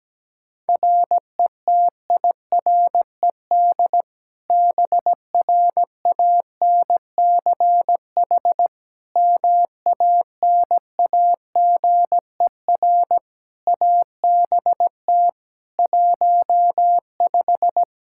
Morse code, heard as a keyed tone, about 17 wpm, 700 Hz. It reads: RETIRED BRANCH MANAGER ABT 15
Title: Practice QSO - Exchange 3 - More Info - 17wpm